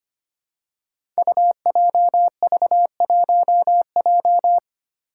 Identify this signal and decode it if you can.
Morse code, heard as a keyed tone, about 25 wpm, 700 Hz. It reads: UJV1J
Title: Five Challenging Letter-Number Combinations 25wpm